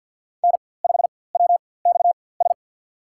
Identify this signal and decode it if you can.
Morse code, heard as a keyed tone, about 51 wpm, spaced as for 23 wpm, 700 Hz. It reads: N5WXS
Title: USA Call Signs - 5 Characters - 4x Character Spacing - 50wpm